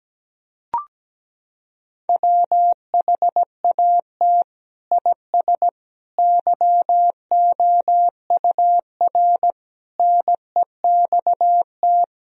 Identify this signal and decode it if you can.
Morse code, heard as a keyed tone, about 17 wpm, 700 Hz. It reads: WHAT IS YOUR NEXT
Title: Sentences from Top 1000 Words - Review Part 1 - 17wpm